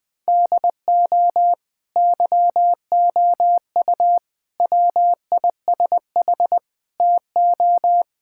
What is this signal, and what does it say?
Morse code, heard as a keyed tone, about 20 wpm, 700 Hz. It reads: DO YOU WISH TO
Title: Sentences from Top 1600 Words 20wpm